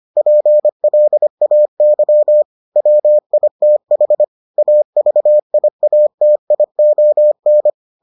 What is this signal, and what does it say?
Morse code, heard as a keyed tone, about 25 wpm, 600 Hz. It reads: PLAY WITH AVIATION